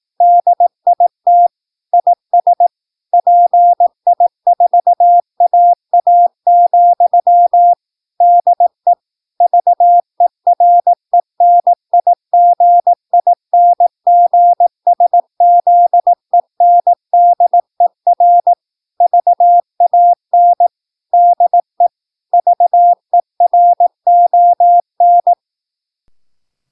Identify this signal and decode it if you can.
Morse code, heard as a keyed tone, about 18 wpm, 700 Hz. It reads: DIT IS PI4AA, DE VERENIGINGSZENDER VAN DE VERON
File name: PI4AA Uitzending